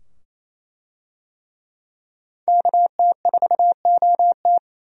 Morse code, heard as a keyed tone, about 27 wpm, 700 Hz. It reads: KT4OT
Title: USA Call Signs - 5 Characters - 28wpm